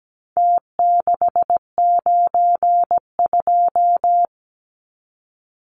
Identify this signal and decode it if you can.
Morse code, heard as a keyed tone, about 17 wpm, 700 Hz. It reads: T692
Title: Numbers - Contest Style - 4-Digit 17wpm